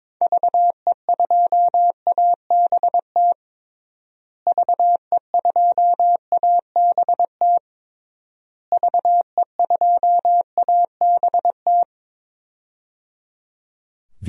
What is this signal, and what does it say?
Morse code, heard as a keyed tone, about 22 wpm, 700 Hz. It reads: VE2ABT VE2ABT VE2ABT
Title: DXCC-100 Call Signs - 6 Characters - Repeated 3x - 22wpm